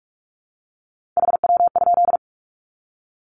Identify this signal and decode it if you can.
Morse code, heard as a keyed tone, about 44 wpm, 700 Hz. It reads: HW?
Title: Common QSO Phrases 45wpm